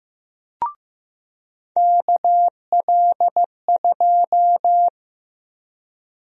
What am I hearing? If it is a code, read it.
Morse code, heard as a keyed tone, about 15 wpm, 700 Hz. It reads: KL2
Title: USA Call Sign Prefixes - 15wpm